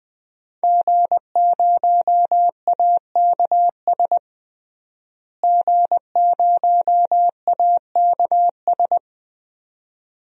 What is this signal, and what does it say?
Morse code, heard as a keyed tone, about 20 wpm, 700 Hz. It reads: G0AKS G0AKS
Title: DXCC-100 Call Signs - 5 Characters - Repeated 2x - 20wpm